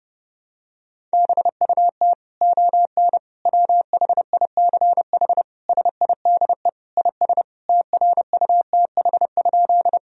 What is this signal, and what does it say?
Morse code, heard as a keyed tone, about 32 wpm, 700 Hz. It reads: BUT ON WHICH SIDE IS TRUTH?
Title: Sentences from Top 1000 Words - Review Part 1 - 30wpm